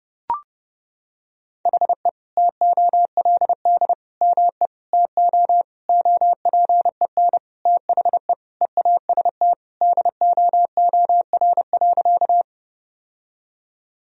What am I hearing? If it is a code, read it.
Morse code, heard as a keyed tone, about 32 wpm, 700 Hz. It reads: HE TOLD ME TO OPEN THE EAST DOOR.
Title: Sentences from Top 1000 Words - Review Part 3 - 30wpm